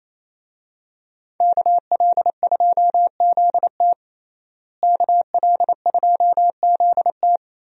Morse code, heard as a keyed tone, about 28 wpm, 700 Hz. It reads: KL2ZT KL2ZT
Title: USA Call Signs - 5 Characters - Repeated 2x - 28wpm